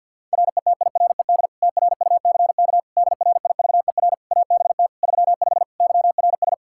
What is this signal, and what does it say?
Morse code, heard as a keyed tone, about 51 wpm, 700 Hz. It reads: RETIRED TRUCK DRIVER ABT 35 YRS
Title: Practice QSO - Exchange 3 - More Info - 50wpm